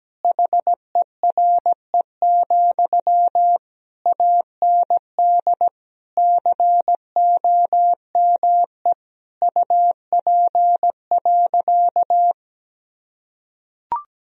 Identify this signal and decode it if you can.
Morse code, heard as a keyed tone, about 17 wpm, 700 Hz. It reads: HERE, AND COME UP.
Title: Sentences from Top 1000 Words - Review Part 2 - 17wpm